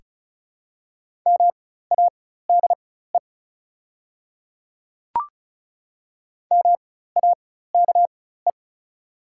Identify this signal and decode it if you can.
Morse code, heard as a keyed tone, about 36 wpm, spaced as for 16 wpm, 700 Hz. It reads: MADE MAKE
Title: 4x Character Spacing - Top 200 Words - 35wpm